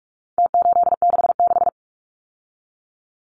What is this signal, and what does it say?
Morse code, heard as a keyed tone, about 44 wpm, 700 Hz. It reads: T866
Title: Numbers - Contest Style - 4-Digit 45wpm